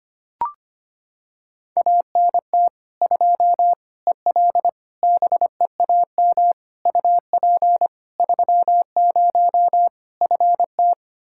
Morse code, heard as a keyed tone, about 25 wpm, 700 Hz. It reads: ANT 2 EL BEAM UP 30 FT
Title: Practice QSO - Exchange 2 - Info - 25wpm